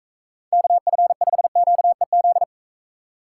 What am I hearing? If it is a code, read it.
Morse code, heard as a keyed tone, about 44 wpm, 700 Hz. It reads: KF5QEZ